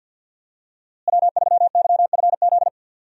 Morse code, heard as a keyed tone, about 48 wpm, 700 Hz. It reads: W3YFZ
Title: USA Call Signs - 5 Characters - Repeated 3x - 50wpm